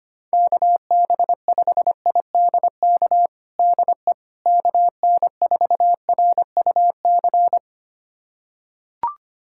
Morse code, heard as a keyed tone, about 25 wpm, 700 Hz. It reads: KB5IDK DE KN4RUC
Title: USA Callsign DE Callsign - 6 Characters - 25wpm